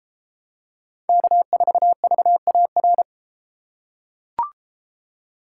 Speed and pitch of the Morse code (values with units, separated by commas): 31 wpm, 700 Hz